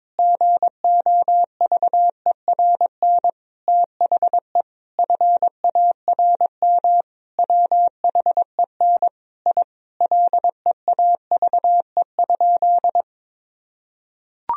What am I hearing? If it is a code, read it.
Morse code, heard as a keyed tone, about 22 wpm, 700 Hz. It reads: GOVERN THE FARM WHEN I LEAVE?
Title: Sentences from Top 1000 Words - Review Part 2 - 22wpm